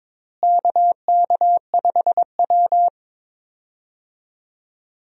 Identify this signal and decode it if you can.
Morse code, heard as a keyed tone, about 21 wpm, 700 Hz. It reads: KK5W
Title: USA Call Signs - 4 Characters - Repeated 2x - 22wpm